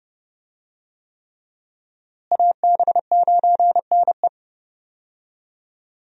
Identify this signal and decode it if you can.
Morse code, heard as a keyed tone, about 31 wpm, 700 Hz. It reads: AB9NE
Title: USA Call Signs - 5 Characters - Repeated 3x - 30wpm